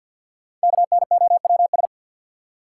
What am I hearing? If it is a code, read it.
Morse code, heard as a keyed tone, about 53 wpm, 700 Hz. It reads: KNOWS
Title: Sentences from Top 1000 Words - Review Part 1 - New Word Drill - 50wpm